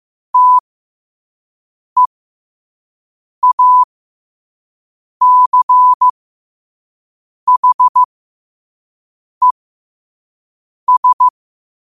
Morse code, heard as a keyed tone, about 15 wpm, spaced as for 5 wpm, 1000 Hz. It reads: TEACHES
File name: 5wpm